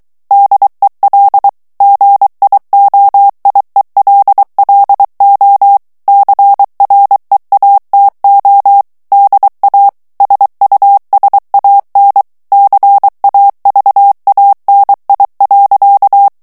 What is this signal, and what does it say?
Morse code, heard as a keyed tone, about 23 wpm, 800 Hz. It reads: DEL GIOIELLO CREATO DA SUSAN CAVANI.
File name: morse mp3 / ITALIEN texte clair 1 vitesse 22 mots minute